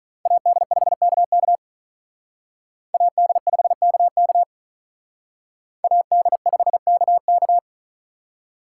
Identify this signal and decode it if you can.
Morse code, heard as a keyed tone, about 40 wpm, 700 Hz. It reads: AD5KK AD5KK AD5KK
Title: Speed Racing - USA 2x2 Call Signs 35wpm